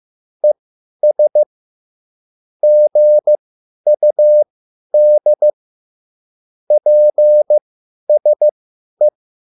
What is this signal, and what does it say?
Morse code, heard as a keyed tone, about 15 wpm, spaced as for 10 wpm, 600 Hz. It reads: ES GUD PSE
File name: QSL / CW lesson